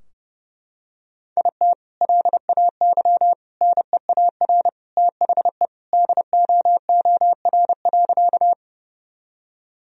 Morse code, heard as a keyed tone, about 30 wpm, 700 Hz. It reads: IT LAY NEAR THE DOOR.
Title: Sentences from Top 1000 Words - Review Part 2 - 30wpm